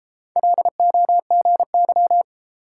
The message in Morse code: LOGY

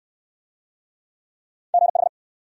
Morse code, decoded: GH

